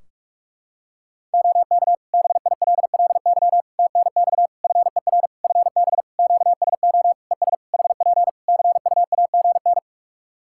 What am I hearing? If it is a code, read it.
Morse code, heard as a keyed tone, about 48 wpm, 700 Hz. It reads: OK BILLY TNX FER FB QSO ES HP CUAGN
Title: Practice QSO - Exchange 4 - Ending - 45wpm